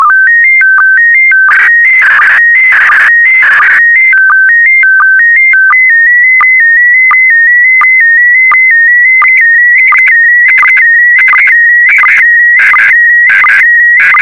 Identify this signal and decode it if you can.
SSTV (slow-scan television), PD 90.
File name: sstv / PD90-VK4MSL